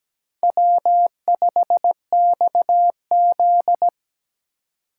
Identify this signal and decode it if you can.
Morse code, heard as a keyed tone, about 20 wpm, 700 Hz. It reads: W5XZ
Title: Speed Racing - USA 1x2 Call Signs 17wpm